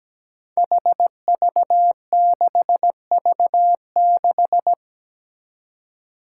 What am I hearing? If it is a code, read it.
Morse code, heard as a keyed tone, about 17 wpm, 700 Hz. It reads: HV6V6
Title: Five Challenging Letter-Number Combinations 17wpm